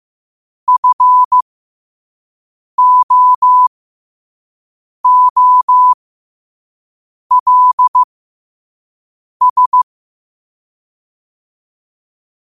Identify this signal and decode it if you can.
Morse code, heard as a keyed tone, about 15 wpm, spaced as for 5 wpm, 1000 Hz. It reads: FOOLS
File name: qotdfiles / 5wpm